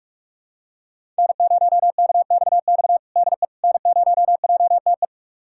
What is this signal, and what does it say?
Morse code, heard as a keyed tone, about 44 wpm, 700 Hz. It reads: N0KXX DE N0JTE
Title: USA Callsign DE Callsign - 5 Characters - 45wpm